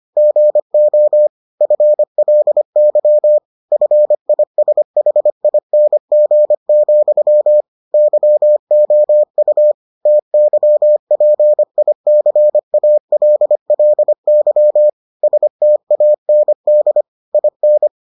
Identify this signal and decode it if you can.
Morse code, heard as a keyed tone, about 25 wpm, 600 Hz. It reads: GO FLY FISHING, YOU TYPICALLY STAND IN